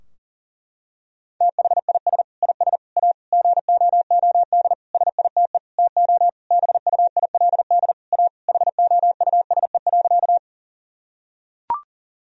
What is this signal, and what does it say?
Morse code, heard as a keyed tone, about 40 wpm, 700 Hz. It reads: THIS IS A GOOD SITE TO BUILD A HOUSE.
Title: Sentences from Top 1000 Words - Review Part 4 - 40wpm